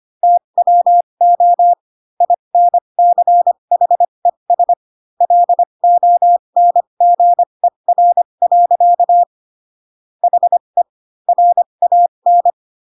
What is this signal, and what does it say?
Morse code, heard as a keyed tone, about 25 wpm, spaced as for 22 wpm, 700 Hz. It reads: TWO INCHES LONGER. HE RAN